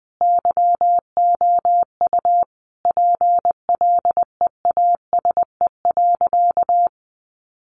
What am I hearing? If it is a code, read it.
Morse code, heard as a keyed tone, about 20 wpm, 700 Hz. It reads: YOU PLEASE.
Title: Sentences from Top 1500 Words 20wpm